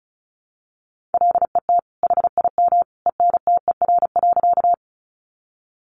Morse code, heard as a keyed tone, about 34 wpm, 700 Hz. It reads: LET HIM ENTER.
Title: Sentences from Top 1300 Words 35wpm